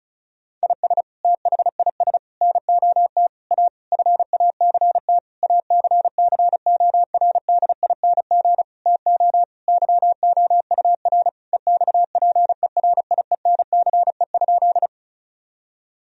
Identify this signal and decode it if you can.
Morse code, heard as a keyed tone, about 36 wpm, 700 Hz. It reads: IS THIS NOT A FACT ACCORDING TO YOUR EXPERIENCE?
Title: Sentences from Top 1000 Words - Review Part 4 - 35wpm